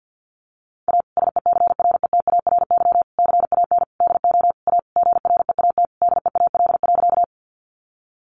Morse code, heard as a keyed tone, about 50 wpm, 700 Hz. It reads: A SECRETARY CAN DO A GREAT DEAL.
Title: Sentences from Top 1300 Words 50wpm